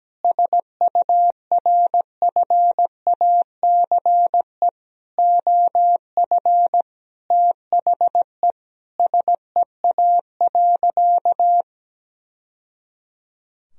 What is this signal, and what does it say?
Morse code, heard as a keyed tone, about 17 wpm, 700 Hz. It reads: SURFACE OF THE SEA.
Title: Sentences from Top 1000 Words - Review Part 3 - 17wpm